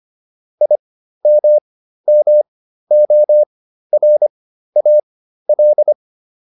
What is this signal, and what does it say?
Morse code, heard as a keyed tone, about 25 wpm, spaced as for 13 wpm, 600 Hz. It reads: IMMORAL